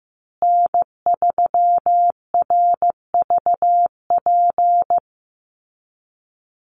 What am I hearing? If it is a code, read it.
Morse code, heard as a keyed tone, about 15 wpm, 700 Hz. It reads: N3RVP
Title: USA Call Signs Repeated 2x 15wpm